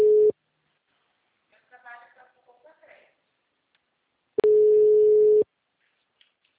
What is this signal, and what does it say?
Ringback tone (Europe, 425 Hz), with speech.